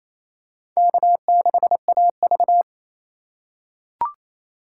Morse code, heard as a keyed tone, about 28 wpm, 700 Hz. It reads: K6AV
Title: USA Call Signs - 4 Characters - 3x Character Spacing - 28wpm